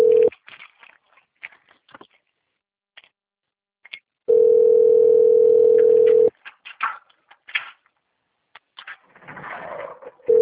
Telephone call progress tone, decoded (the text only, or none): ringback tone